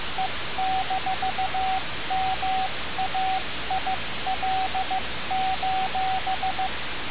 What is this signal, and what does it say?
Morse code, heard as a keyed tone, about 15 wpm, 760 Hz, with radio noise. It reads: E-MAIL:
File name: radiocap / REC z